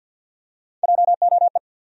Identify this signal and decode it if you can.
Morse code, heard as a keyed tone, about 53 wpm, 700 Hz. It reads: JOE
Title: USA Common Names - 50wpm